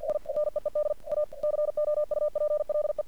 Morse code, heard as a keyed tone, about 48 wpm, 600 Hz. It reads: I NEED A COUPLE
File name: archived / CRSnet